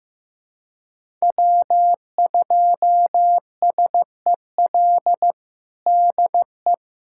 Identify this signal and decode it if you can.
Morse code, heard as a keyed tone, about 15 wpm, 700 Hz. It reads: W2SEL DE
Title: USA Callsign DE Callsign - 5 Characters - 15wpm